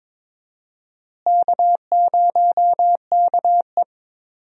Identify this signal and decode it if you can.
Morse code, heard as a keyed tone, about 22 wpm, 700 Hz. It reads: K0KE
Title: USA Call Signs - 4 Characters - Repeated 2x - 22wpm